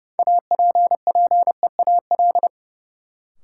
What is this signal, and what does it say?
Morse code, heard as a keyed tone, about 30 wpm, 700 Hz. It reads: APPEAL
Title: Sentences from Top 1300 Words - New Word Drill - 30wpm